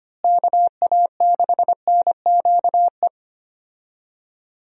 Morse code, heard as a keyed tone, about 25 wpm, 700 Hz. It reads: KA6NQE
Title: USA Call Signs - 6 Characters - Repeated 3x - 25wpm